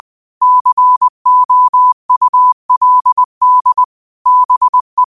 Morse code, heard as a keyed tone, about 20 wpm, 1000 Hz. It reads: COULD BE